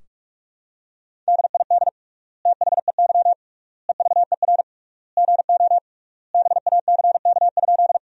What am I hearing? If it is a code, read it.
Morse code, heard as a keyed tone, about 46 wpm, 700 Hz. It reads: DID THEY EVER GO BACK?
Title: 3x Word-Spacing - Sentences from Top 400 Words - 45wpm